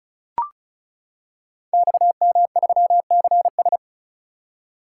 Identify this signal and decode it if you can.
Morse code, heard as a keyed tone, about 36 wpm, 700 Hz. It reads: XM3CS
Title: Code Groups - 5 Letter:Numbers - 35wpm